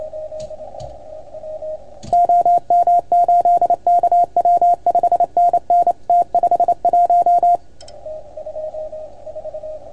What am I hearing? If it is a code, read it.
Morse code, heard as a keyed tone, about 28 wpm, 690 Hz. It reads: OM8KW 5NN T51
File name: Archiv / CUC310 / OK1FGD CUC 310 pileup